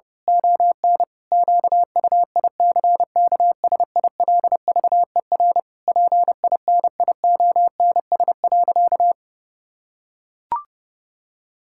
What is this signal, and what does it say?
Morse code, heard as a keyed tone, about 30 wpm, 700 Hz. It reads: ON QUICKSILVER PINIONS.
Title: Pangram Challenge v2 - 30wpm